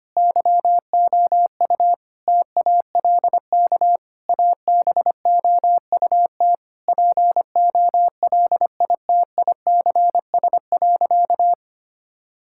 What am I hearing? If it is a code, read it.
Morse code, heard as a keyed tone, about 25 wpm, 700 Hz. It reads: YOU TALK ABOUT POLITICS.
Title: Sentences from Top 1000 Words - Review Part 6 - 25wpm